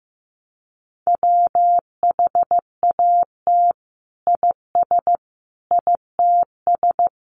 Morse code, heard as a keyed tone, about 15 wpm, 700 Hz. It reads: WHAT IS ITS
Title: Sentences from Top 1100 Words 15wpm